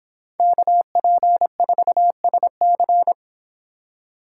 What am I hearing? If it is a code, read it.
Morse code, heard as a keyed tone, about 26 wpm, 700 Hz. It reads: KP4SC